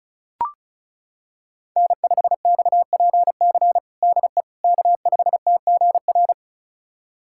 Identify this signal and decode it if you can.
Morse code, heard as a keyed tone, about 36 wpm, 700 Hz. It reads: N5XPC DE K5TGR
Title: USA Callsign DE Callsign - 5 Characters - 35wpm